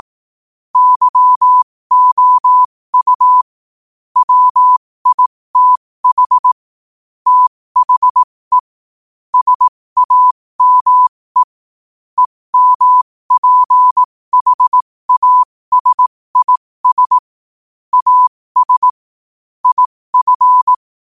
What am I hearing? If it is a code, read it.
Morse code, heard as a keyed tone, about 18 wpm, spaced as for 15 wpm, 1000 Hz. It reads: YOU WITH THE SAME EMPHASIS AS IF
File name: qotdfiles / 15wpm